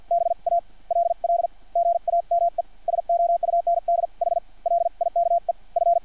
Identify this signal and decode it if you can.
Morse code, heard as a keyed tone, about 50 wpm, spaced as for 40 wpm, 690 Hz. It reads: BALL GAME SOUNDH LIKE F
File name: archived / CRSnet